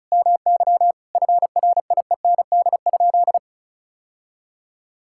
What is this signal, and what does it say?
Morse code, heard as a keyed tone, about 36 wpm, 700 Hz. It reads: MY FRIEND?
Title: Sentences from Top 1000 Words - Review Part 5 - 35wpm